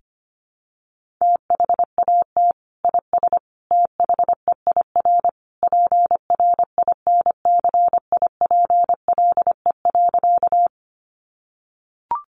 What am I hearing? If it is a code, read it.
Morse code, heard as a keyed tone, about 25 wpm, 700 Hz. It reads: THAT IS THEIR PRINCIPLE.
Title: Sentences from Top 1400 Words 25wpm